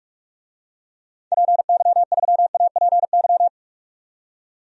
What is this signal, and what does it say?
Morse code, heard as a keyed tone, about 44 wpm, 690 Hz. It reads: PY3APY
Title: DXCC-100 Call Signs - 6 Characters - 45wpm